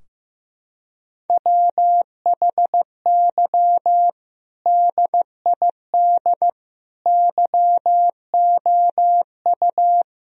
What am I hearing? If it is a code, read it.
Morse code, heard as a keyed tone, about 15 wpm, 700 Hz. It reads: WHY DID YOU
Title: Sentences from Top 1000 Words - Review Part 3 - 15wpm